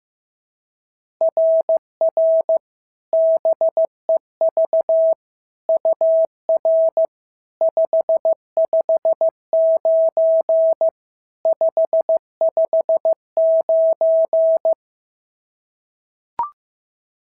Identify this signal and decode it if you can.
Morse code, heard as a keyed tone, about 14 wpm, 650 Hz. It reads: RR BEV UR 559 559
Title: SOTA QSO Practice 15wpm